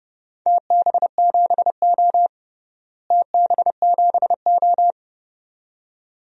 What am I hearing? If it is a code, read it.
Morse code, heard as a keyed tone, about 31 wpm, 700 Hz. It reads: TB7O TB7O
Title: DXCC-100 Call Signs - 4 Characters - Repeated 2x - 30wpm